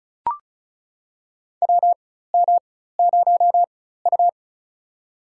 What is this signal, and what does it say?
Morse code, heard as a keyed tone, about 36 wpm, spaced as for 16 wpm, 700 Hz. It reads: WM0U
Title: USA Call Signs - 4 Characters - 4x Character Spacing - 35wpm